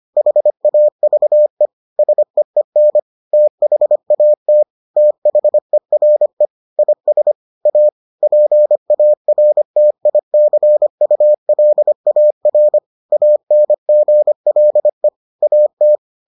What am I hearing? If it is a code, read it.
Morse code, heard as a keyed tone, about 25 wpm, 600 Hz. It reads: HAVE SEEN THAT THERE IS A PARTICULAR ANGLE AT